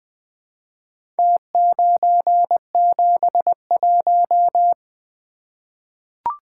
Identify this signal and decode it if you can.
Morse code, heard as a keyed tone, about 20 wpm, 700 Hz. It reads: T971
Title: Four Letter-Number Combinations - 20wpm